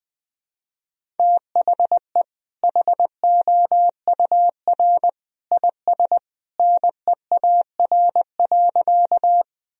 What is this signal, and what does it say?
Morse code, heard as a keyed tone, about 20 wpm, 700 Hz. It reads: THE HOUR IS NEAR.
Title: Sentences from Top 1000 Words - Review Part 1 - 20wpm